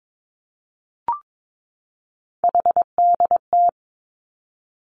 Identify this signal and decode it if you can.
Morse code, heard as a keyed tone, about 22 wpm, 700 Hz. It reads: HDT
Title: Three Challenging Letter-Number Combinations 22wpm